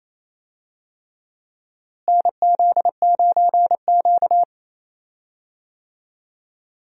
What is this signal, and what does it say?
Morse code, heard as a keyed tone, about 27 wpm, 700 Hz. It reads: NZ9Q
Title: USA Call Signs - 4 Characters - Repeated 3x - 28wpm